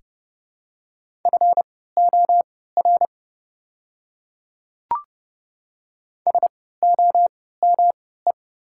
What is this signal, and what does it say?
Morse code, heard as a keyed tone, about 29 wpm, spaced as for 17 wpm, 700 Hz. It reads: FOR SOME
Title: 3x Character Spacing - Top 100 Words - 30wpm